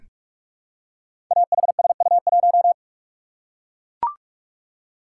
Morse code, heard as a keyed tone, about 44 wpm, 700 Hz. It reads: AHSU1
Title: Code Groups - 5 Letter:Numbers - 45wpm